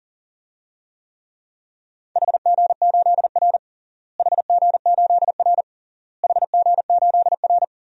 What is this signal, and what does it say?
Morse code, heard as a keyed tone, about 40 wpm, 700 Hz. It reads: HG8R HG8R HG8R
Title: DXCC-100 Call Signs - 4 Characters - Repeated 3x - 40wpm